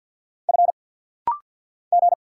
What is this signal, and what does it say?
Morse code, heard as a keyed tone, about 50 wpm, 690 Hz. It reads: F G
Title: Single Letter-Number - Rapid-Fire 50wpm